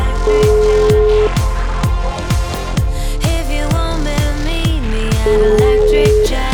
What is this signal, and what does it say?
Ringback tone (Europe, 425 Hz), with noise.